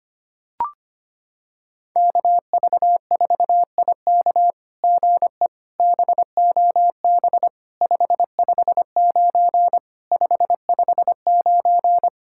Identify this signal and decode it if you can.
Morse code, heard as a keyed tone, about 25 wpm, 700 Hz. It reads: KV4IK GE BOB 559 559
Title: SOTA QSO Practice 25wpm